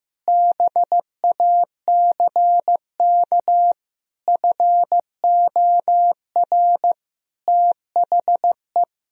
Morse code, heard as a keyed tone, about 15 wpm, 700 Hz. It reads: BACK FOR THE
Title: Sentences from Top 1000 Words - Review Part 4 - 15wpm